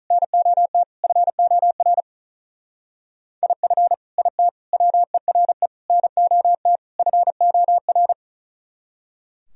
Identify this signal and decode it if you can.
Morse code, heard as a keyed tone, about 42 wpm, 700 Hz. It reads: NOT FOR IF IT WERE NOT FOR
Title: Speed-Racing - Sets of 5 Words 35wpm